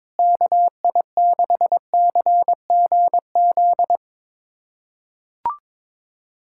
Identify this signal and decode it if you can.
Morse code, heard as a keyed tone, about 22 wpm, 700 Hz. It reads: KI6CGZ
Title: USA Callsign DE Callsign - 6 Characters - 22wpm